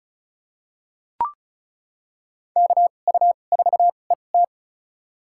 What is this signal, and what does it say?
Morse code, heard as a keyed tone, about 34 wpm, spaced as for 25 wpm, 700 Hz. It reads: KU4ET
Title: USA Call Signs - 2x Character Spacing - 35wpm